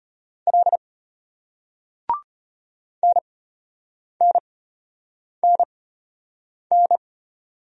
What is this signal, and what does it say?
Morse code, heard as a keyed tone, about 33 wpm, spaced as for 8 wpm, 700 Hz. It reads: L NNNN